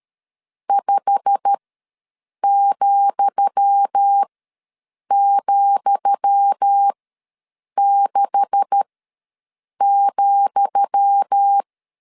Morse code, heard as a keyed tone, about 13 wpm, spaced as for 7 wpm, 800 Hz. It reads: 5,,6,
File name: LEZIONI / 10 SIGNS / 10D COMA ws 12wpm 3618